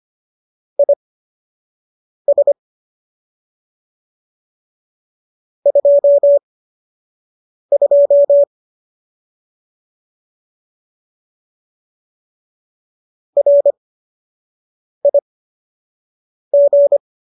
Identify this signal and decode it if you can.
Morse code, heard as a keyed tone, about 25 wpm, spaced as for 6 wpm, 570 Hz. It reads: IS 22 RIG